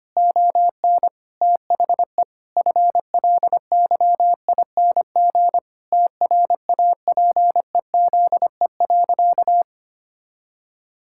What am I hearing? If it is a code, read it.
Morse code, heard as a keyed tone, about 25 wpm, 700 Hz. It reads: ON THE FLYING TRAPEZE.
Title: Pangram Challenge v2 - 25wpm